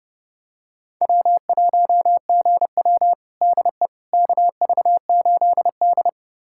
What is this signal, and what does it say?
Morse code, heard as a keyed tone, about 30 wpm, 700 Hz. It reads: W1GW DE KV8D
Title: USA Callsign DE Callsign - 4 Characters - 30wpm